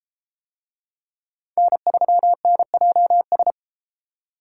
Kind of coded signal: Morse code

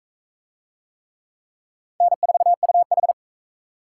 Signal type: Morse code

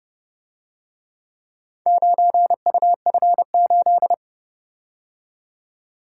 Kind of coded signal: Morse code

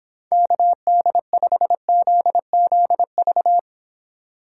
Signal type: Morse code

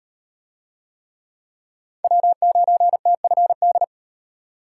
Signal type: Morse code

